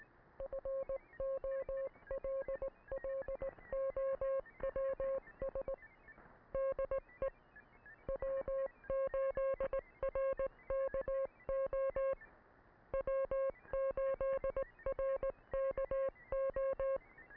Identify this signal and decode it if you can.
Morse code, heard as a keyed tone, about 18 wpm, 540 Hz. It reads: FOLLOWS DE W8RKO W8RKO